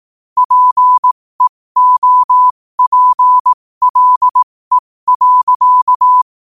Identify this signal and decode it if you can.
Morse code, heard as a keyed tone, about 18 wpm, spaced as for 15 wpm, 1000 Hz. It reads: PEOPLE.
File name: qotdfiles / 15wpm